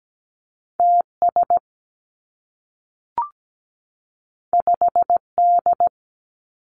Morse code, heard as a keyed tone, about 17 wpm, 700 Hz. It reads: TS 5D
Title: Two Challenging Letter-Number Combinations 17wpm